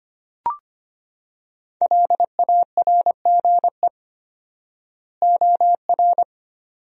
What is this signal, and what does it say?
Morse code, heard as a keyed tone, about 25 wpm, 700 Hz. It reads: LARGE OR
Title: 4x Word-Spacing - Sentences from Top 200 Words - 25wpm